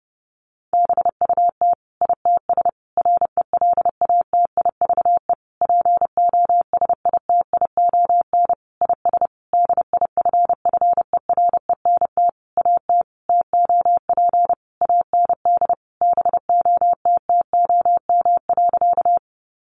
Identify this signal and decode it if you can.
Morse code, heard as a keyed tone, about 30 wpm, 700 Hz. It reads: BUT ITS RELATIVE POSITION IS DIFFERENT AT TOP AND BOTTOM.
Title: Sentences from Top 1400 Words 30wpm